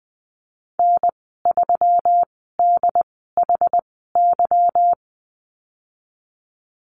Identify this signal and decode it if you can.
Morse code, heard as a keyed tone, about 20 wpm, spaced as for 15 wpm, 700 Hz. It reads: N3DHY
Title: 147 Lesson 24g - Letter y - Call Signs 20wpm